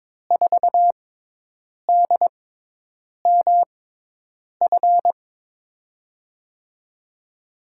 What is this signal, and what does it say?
Morse code, heard as a keyed tone, about 22 wpm, spaced as for 8 wpm, 700 Hz. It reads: 4DMF
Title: USA Call Signs - 5 Characters - 6x Character Spacing - 22wpm